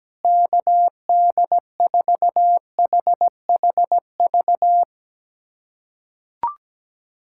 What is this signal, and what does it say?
Morse code, heard as a keyed tone, about 17 wpm, 700 Hz. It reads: KD4HHV
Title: USA Callsign DE Callsign 17wpm